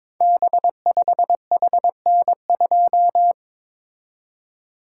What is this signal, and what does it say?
Morse code, heard as a keyed tone, about 22 wpm, 700 Hz. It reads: B5HN2
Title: Code Groups - 5 Letter:Numbers - 22wpm